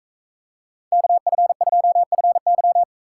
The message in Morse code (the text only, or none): KF2FY